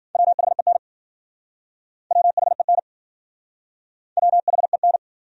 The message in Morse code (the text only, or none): WHEN WHEN WHEN